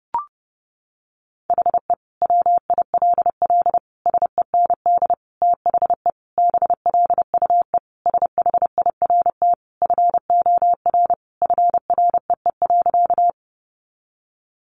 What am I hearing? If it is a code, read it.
Morse code, heard as a keyed tone, about 30 wpm, 700 Hz. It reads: HE WILL SEND THE BLUE SHIRT FOR FREE.
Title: Additional Easy Phrases 30wpm